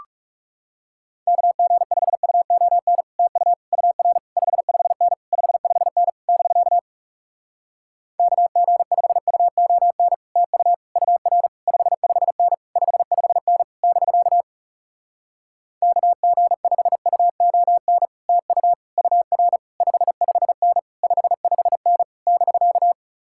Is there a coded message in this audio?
Yes: Morse code from 1.3 to 22.9 s, heard as a keyed tone, about 40 wpm, 700 Hz. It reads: KG5UON TU UR 55N 55N <BK> KG5UON TU UR 55N 55N <BK> KG5UON TU UR 55N 55N <BK>